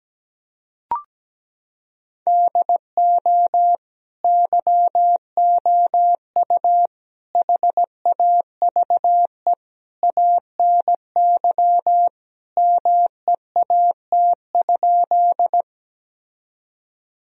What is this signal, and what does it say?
Morse code, heard as a keyed tone, about 17 wpm, 700 Hz. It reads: DO YOU HAVE ANY MEAT?
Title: Sentences from Top 1000 Words - Review Part 3 - 17wpm